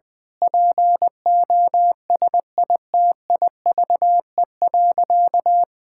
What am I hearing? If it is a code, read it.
Morse code, heard as a keyed tone, about 20 wpm, 700 Hz. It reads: POSITIVE.
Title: Sentences with 8-Word Phrases - 20wpm